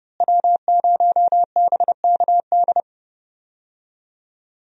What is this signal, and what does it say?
Morse code, heard as a keyed tone, about 32 wpm, 700 Hz. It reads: W0BKD
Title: USA Call Signs - 5 Characters - Repeated 2x - 30wpm